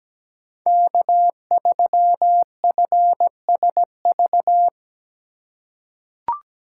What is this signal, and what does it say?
Morse code, heard as a keyed tone, about 17 wpm, 700 Hz. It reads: K3FSV
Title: USA Call Signs - 5 Characters - 3x Character Spacing - 17wpm